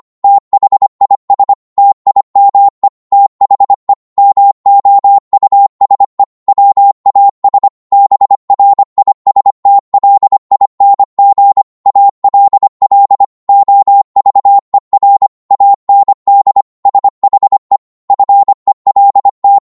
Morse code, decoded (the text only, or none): THIS TIME THE MOUSE WAS BRISTLING ALL OVER AND SHE FELT